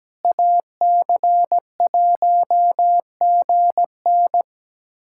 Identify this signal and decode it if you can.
Morse code, heard as a keyed tone, about 17 wpm, 700 Hz. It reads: AC1GN
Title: USA Callsign DE Callsign - 5 Characters - 17wpm